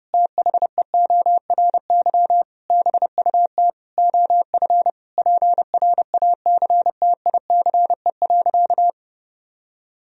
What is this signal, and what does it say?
Morse code, heard as a keyed tone, about 30 wpm, 700 Hz. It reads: THEORY BUT OF PRACTICE.
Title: Sentences from Top 1000 Words - Review Part 4 - 30wpm